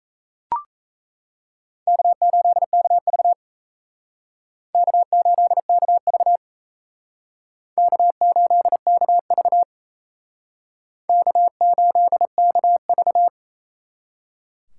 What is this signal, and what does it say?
Morse code, heard as a keyed tone, about 35 wpm, 700 Hz. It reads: K8KV K8KV K8KV K8KV
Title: Speed Racing - USA 1x2 Call Signs 28wpm